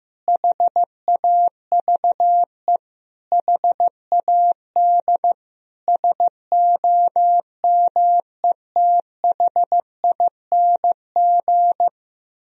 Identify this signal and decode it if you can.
Morse code, heard as a keyed tone, about 15 wpm, 700 Hz. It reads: HAVE HAD SOMETHING
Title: Sentences from Top 1000 Words - Review Part 2 - 15wpm